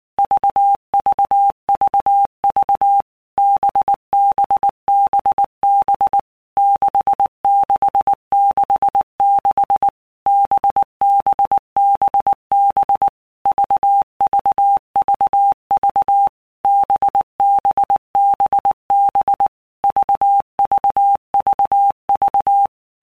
Morse code, heard as a keyed tone, about 19 wpm, 800 Hz. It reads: VVVV BBBB 6666 BBBB VVVV BBBB VVVV
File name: Basic Letter Groups By Blocks / Groups@18 / VB46@18